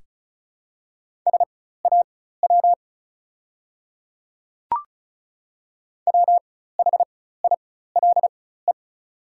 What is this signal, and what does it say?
Morse code, heard as a keyed tone, about 36 wpm, spaced as for 16 wpm, 700 Hz. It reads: SAW WHILE
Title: 4x Character Spacing - Top 300 Words - 35wpm